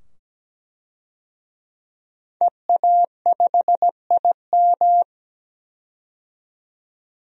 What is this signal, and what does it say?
Morse code, heard as a keyed tone, about 17 wpm, 700 Hz. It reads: EA5IM
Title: DXCC-100 Call Signs - 5 Characters - 17wpm